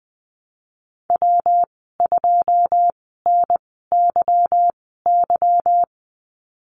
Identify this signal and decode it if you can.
Morse code, heard as a keyed tone, about 20 wpm, spaced as for 15 wpm, 700 Hz. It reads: W2NYY
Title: 167 Lesson 27g - Number 7 - Call Signs 20wpm